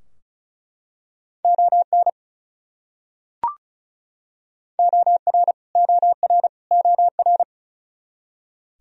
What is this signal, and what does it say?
Morse code, heard as a keyed tone, about 36 wpm, 700 Hz. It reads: ON OR OR OR
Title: Top 100 Words Repeated 35wpm